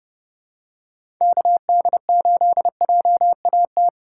Morse code, heard as a keyed tone, about 31 wpm, 700 Hz. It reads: KD8JAT
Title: USA Call Signs - 6 Characters - 6x Character Spacing - 30wpm